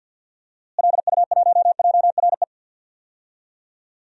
Morse code, heard as a keyed tone, about 51 wpm, 700 Hz. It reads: LU1JRE
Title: DXCC-100 Call Signs - 6 Characters - Repeated 3x - 50wpm